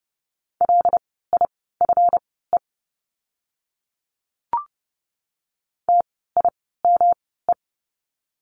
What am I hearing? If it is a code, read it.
Morse code, heard as a keyed tone, about 30 wpm, spaced as for 17 wpm, 700 Hz. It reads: LIFE TIME
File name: course / 30:10wpm / 237 Lesson 38f - Letter q - Words 30wpm